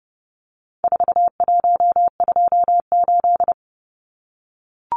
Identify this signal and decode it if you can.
Morse code, heard as a keyed tone, about 30 wpm, 700 Hz. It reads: 4128
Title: Numbers - 4-Digit 30wpm